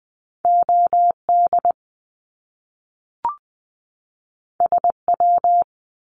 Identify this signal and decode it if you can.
Morse code, heard as a keyed tone, about 20 wpm, 700 Hz. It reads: OD SW
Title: Top Two Letter Combinations in Words 20wpm